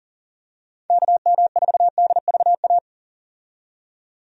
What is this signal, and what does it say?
Morse code, heard as a keyed tone, about 40 wpm, 700 Hz. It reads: KM4DVA
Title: USA Call Signs - 6 Characters - 40wpm